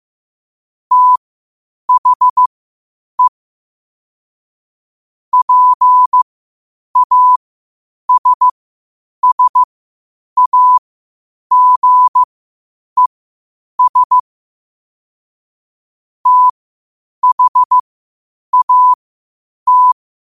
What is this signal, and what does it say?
Morse code, heard as a keyed tone, about 15 wpm, spaced as for 8 wpm, 1000 Hz. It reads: THE PASSAGES THAT
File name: qotdfiles / 8wpm